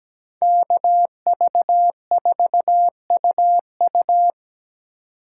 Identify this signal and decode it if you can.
Morse code, heard as a keyed tone, about 21 wpm, 700 Hz. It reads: KV4UU
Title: Speed Racing - USA 2x2 Call Signs 17wpm